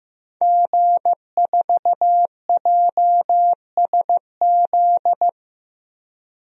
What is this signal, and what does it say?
Morse code, heard as a keyed tone, about 15 wpm, 700 Hz. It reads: G4JSZ
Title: DXCC-100 Call Signs - 5 Characters - Repeated 3x - 15wpm